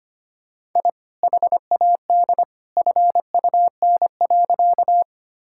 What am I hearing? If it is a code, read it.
Morse code, heard as a keyed tone, about 25 wpm, 700 Hz. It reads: I HAD FUN.
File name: practice / 25wpm / Sentences with 2-Word Phrases - 25wpm